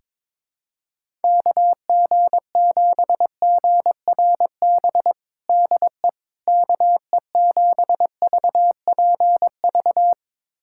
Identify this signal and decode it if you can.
Morse code, heard as a keyed tone, about 22 wpm, 700 Hz. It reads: KG7GRB DE KE7VPV
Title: USA Callsign DE Callsign - 6 Characters - 22wpm